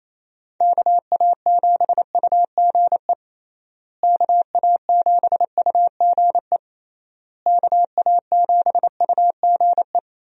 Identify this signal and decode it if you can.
Morse code, heard as a keyed tone, about 27 wpm, 700 Hz. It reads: KA7UGE KA7UGE KA7UGE
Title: USA Call Signs - 6 Characters - Repeated 3x - 28wpm